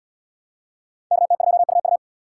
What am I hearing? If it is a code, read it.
Morse code, heard as a keyed tone, about 70 wpm, 700 Hz. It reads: KT2UU